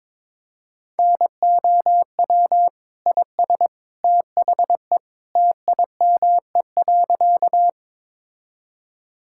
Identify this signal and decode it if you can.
Morse code, heard as a keyed tone, about 22 wpm, 700 Hz. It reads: NOW IS THE TIME.
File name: practice / 22wpm / Sentences with 2-Word Phrases - 22wpm